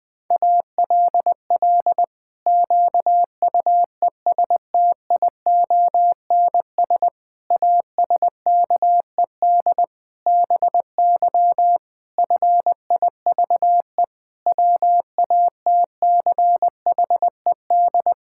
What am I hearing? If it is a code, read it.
Morse code, heard as a keyed tone, about 20 wpm, 700 Hz. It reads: ALL QUESTIONS ASKED BY FIVE WATCHED
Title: Pangram Challenge v2 - 20wpm